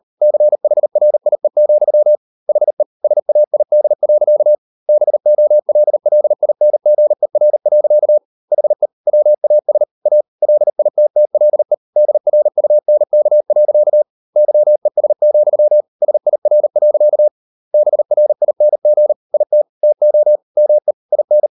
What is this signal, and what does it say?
Morse code, heard as a keyed tone, about 38 wpm, 600 Hz. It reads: CHRIE, HE SAID. BOLLINGER. HE WAS A LITTLE DRUNK. YES, SIR. BRING IT TO ME IN